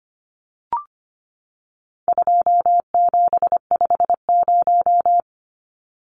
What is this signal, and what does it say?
Morse code, heard as a keyed tone, about 25 wpm, 700 Hz. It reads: 2750
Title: Numbers - 4-Digit 25wpm